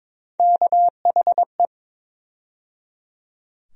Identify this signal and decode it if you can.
Morse code, heard as a keyed tone, about 22 wpm, 700 Hz. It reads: KHE
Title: USA Call Sign Postfixes - 3 Letters 22wpm